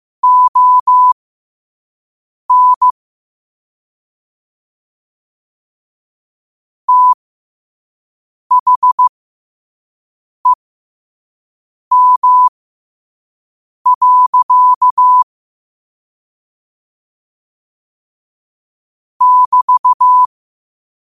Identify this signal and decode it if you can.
Morse code, heard as a keyed tone, about 15 wpm, spaced as for 5 wpm, 1000 Hz. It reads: ON THEM. <BT>
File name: qotdfiles / 5wpm